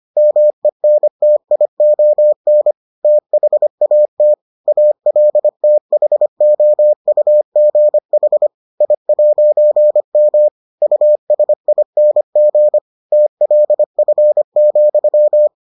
Morse code, heard as a keyed tone, about 25 wpm, 600 Hz. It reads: MENTION THAT ALTHOUGH I'M USING TLF,